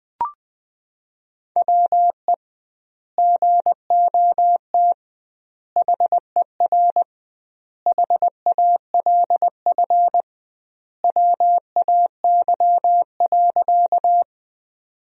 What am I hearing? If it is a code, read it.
Morse code, heard as a keyed tone, about 20 wpm, 700 Hz. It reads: WE GOT HER HALF WAY.
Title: 2x Word-Spacing - Sentences from Top 400 Words - 20wpm